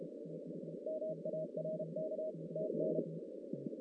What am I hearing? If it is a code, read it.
Morse code, heard as a keyed tone, about 33 wpm, 630 Hz, with noise. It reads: MARK TN
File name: 2016 JAN CW NAQP / 2016 Jan naqp k5ot - 0-9